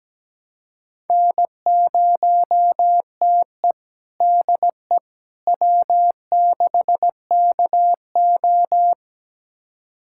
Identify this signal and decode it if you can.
Morse code, heard as a keyed tone, about 17 wpm, 700 Hz. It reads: N0TE DE W6KO
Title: USA Callsign DE Callsign - 4 Characters - 17wpm